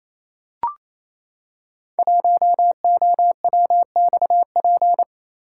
Morse code, heard as a keyed tone, about 28 wpm, 700 Hz. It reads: 1OWXP
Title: Code Groups - 5 Letter:Numbers - 28wpm